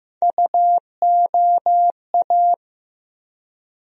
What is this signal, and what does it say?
Morse code, heard as a keyed tone, about 15 wpm, 700 Hz. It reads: UOA
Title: USA Call Sign Postfixes - 3 Letters 15wpm